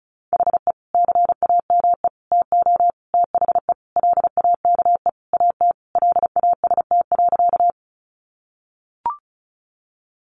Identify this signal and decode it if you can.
Morse code, heard as a keyed tone, about 34 wpm, 700 Hz. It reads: HE CAME TO THE LAKE AT LAST.
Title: Sentences from Top 1100 Words 35wpm